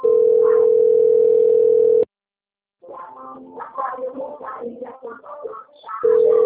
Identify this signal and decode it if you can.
Ringback tone (North America), 440 + 480 Hz.